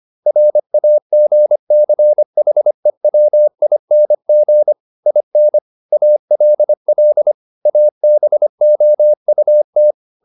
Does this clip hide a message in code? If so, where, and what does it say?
Yes: Morse code from 0.3 to 9.9 s, heard as a keyed tone, about 25 wpm, 600 Hz. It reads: RAGCHEWING IN ALL ABOUT